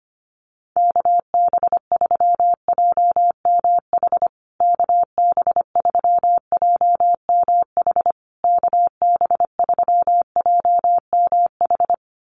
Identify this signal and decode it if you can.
Morse code, heard as a keyed tone, about 25 wpm, 700 Hz. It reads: KB3JMH KB3JMH KB3JMH
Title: USA Call Signs Repeated 3x 25wpm